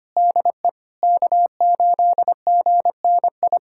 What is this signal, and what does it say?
Morse code, heard as a keyed tone, about 25 wpm, 700 Hz. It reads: DE K8GNI
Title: USA Callsign DE Callsign - 5 Characters - 25wpm